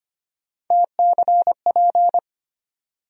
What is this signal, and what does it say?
Morse code, heard as a keyed tone, about 25 wpm, 700 Hz. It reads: TCP
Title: USA Call Sign Postfixes - 3 Letters 25wpm